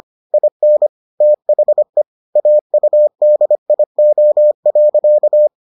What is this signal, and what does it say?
Morse code, heard as a keyed tone, about 25 wpm, 600 Hz. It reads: IN THE AUDIO.